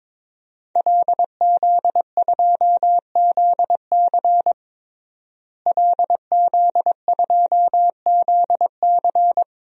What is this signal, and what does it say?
Morse code, heard as a keyed tone, about 22 wpm, 700 Hz. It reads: LZ2ZC LZ2ZC
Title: DXCC-100 Call Signs - 5 Characters - Repeated 2x - 22wpm